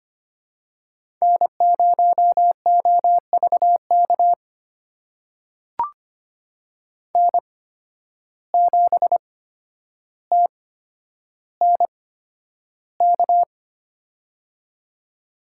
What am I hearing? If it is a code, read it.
Morse code, heard as a keyed tone, about 25 wpm, spaced as for 11 wpm, 700 Hz. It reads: N0OVK N7TNK
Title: USA Call Signs - 5 Characters - 8x Character Spacing - 25wpm